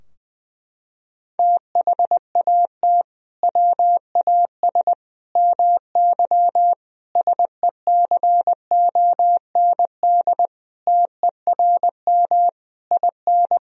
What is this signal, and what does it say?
Morse code, heard as a keyed tone, about 20 wpm, 700 Hz. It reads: THAT WAS MY SECOND TERM IN
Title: Sentences from Top 1000 Words - Review Part 4 - 20wpm